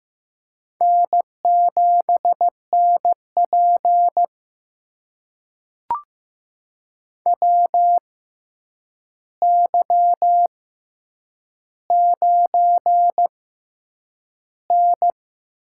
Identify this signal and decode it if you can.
Morse code, heard as a keyed tone, about 15 wpm, spaced as for 8 wpm, 700 Hz. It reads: N7NP WY9N
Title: USA Call Signs - 4 Characters - 6x Character Spacing - 15wpm